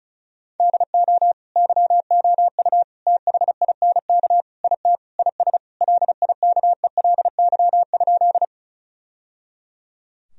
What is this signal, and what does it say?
Morse code, heard as a keyed tone, about 36 wpm, 700 Hz. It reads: DO YOU THINK IT IS LIKELY?
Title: Sentences from Top 1000 Words - Review Part 4 - 35wpm